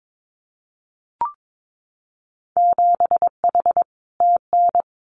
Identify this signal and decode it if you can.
Morse code, heard as a keyed tone, about 22 wpm, 700 Hz. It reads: 7H TN
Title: Winter Field Day Exchanges Encore 22wpm